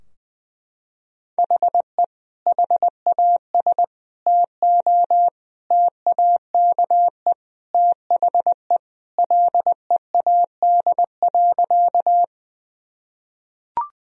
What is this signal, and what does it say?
Morse code, heard as a keyed tone, about 20 wpm, 700 Hz. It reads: HE HAS TO TAKE THE LEAD.
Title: Sentences from Top 1000 Words - Review Part 2 - 20wpm